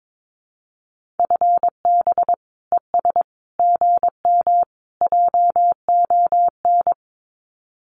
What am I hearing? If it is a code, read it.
Morse code, heard as a keyed tone, about 22 wpm, 700 Hz. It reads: FB ES GM JON
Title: Common QSO Phrases Encore 22wpm